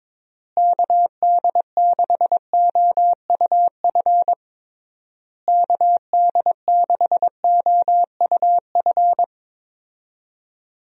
Morse code, heard as a keyed tone, about 22 wpm, 700 Hz. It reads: KD6OUF KD6OUF
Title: USA Call Signs - 6 Characters - Repeated 3x - 22wpm